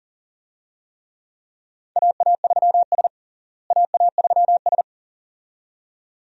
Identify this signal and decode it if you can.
Morse code, heard as a keyed tone, about 40 wpm, 700 Hz. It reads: AA3S AA3S
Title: USA Call Signs - 4 Characters - Repeated 2x - 40wpm